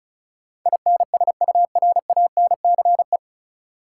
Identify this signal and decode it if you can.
Morse code, heard as a keyed tone, about 36 wpm, 700 Hz. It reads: INSURANCE
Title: Sentences from Top 1100 Words - New Word Drill Alt - 35wpm